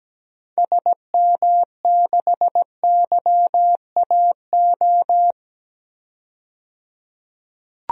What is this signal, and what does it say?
Morse code, heard as a keyed tone, about 17 wpm, 700 Hz. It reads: SM6YAO
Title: DXCC-100 Call Signs - 6 Characters - 17wpm